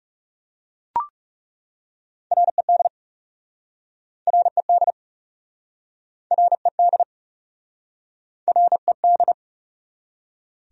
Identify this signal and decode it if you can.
Morse code, heard as a keyed tone, about 41 wpm, 700 Hz. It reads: RED RED RED RED